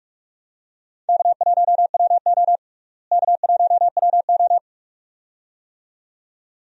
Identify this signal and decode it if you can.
Morse code, heard as a keyed tone, about 42 wpm, 700 Hz. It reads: K1WO K1WO
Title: USA Call Signs - 4 Characters - Repeated 2x - 45wpm